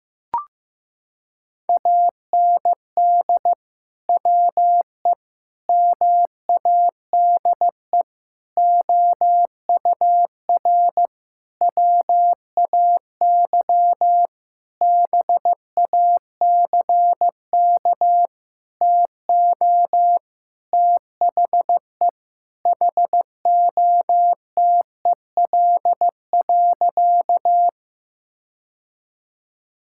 Morse code, heard as a keyed tone, about 15 wpm, 700 Hz. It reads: AND WE MADE OUR WAY BACK TO THE HOTEL.
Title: Sentences from Top 1600 Words 15wpm